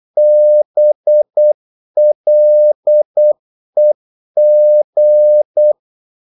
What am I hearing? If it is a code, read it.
Morse code, heard as a keyed tone, about 8 wpm, 600 Hz. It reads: BLEG
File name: Korte ord DK 8wpm